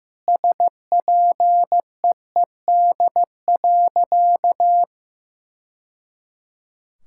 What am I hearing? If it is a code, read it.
Morse code, heard as a keyed tone, about 15 wpm, 700 Hz. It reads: SPEED.
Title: Sentences from Top 1000 Words - Review Part 5 - 15wpm